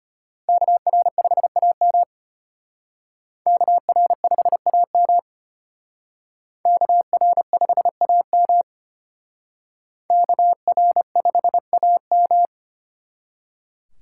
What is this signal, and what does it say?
Morse code, heard as a keyed tone, about 32 wpm, 700 Hz. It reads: KR5AM KR5AM KR5AM KR5AM